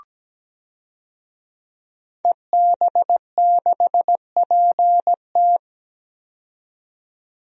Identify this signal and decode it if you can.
Morse code, heard as a keyed tone, about 17 wpm, 700 Hz. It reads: EB6PT
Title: DXCC-100 Call Signs - 5 Characters - 17wpm